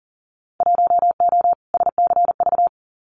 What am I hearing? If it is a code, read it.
Morse code, heard as a keyed tone, about 41 wpm, 700 Hz. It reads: 1O SCV
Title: Winter Field Day Exchanges 40wpm